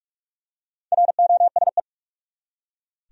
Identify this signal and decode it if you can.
Morse code, heard as a keyed tone, about 43 wpm, 700 Hz. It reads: ROSE
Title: USA Common Names - 45wpm